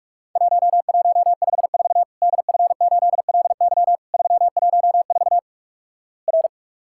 Morse code, heard as a keyed tone, about 46 wpm, 700 Hz. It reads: 1154 DF8LY 314 R
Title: Worked All Europe DX Contest - QTC Exchange - 45wpm